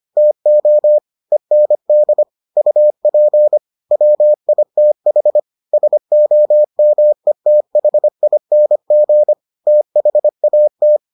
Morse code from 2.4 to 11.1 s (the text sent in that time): UP WITH SOMETHING THAT